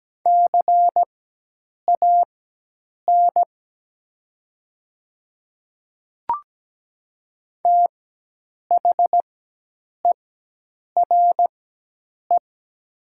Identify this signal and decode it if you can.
Morse code, heard as a keyed tone, about 17 wpm, spaced as for 8 wpm, 700 Hz. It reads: CAN THERE
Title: 4x Character Spacing - Top 100 Words - 17wpm